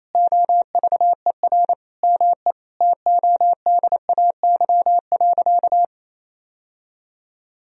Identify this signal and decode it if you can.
Morse code, heard as a keyed tone, about 28 wpm, 700 Hz. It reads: OVER ME TODAY.
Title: Sentences from Top 1000 Words - Review Part 4 - 28wpm